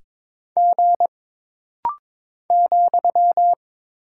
Morse code, heard as a keyed tone, about 22 wpm, 700 Hz. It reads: G ,
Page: This rapid-fire timing is geared to push your ICR (Instant Character Recognition) skills.